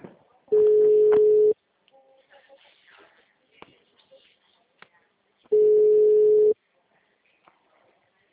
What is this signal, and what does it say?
Ringback tone (Europe, 425 Hz), with speech.
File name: out- -2009- - - .73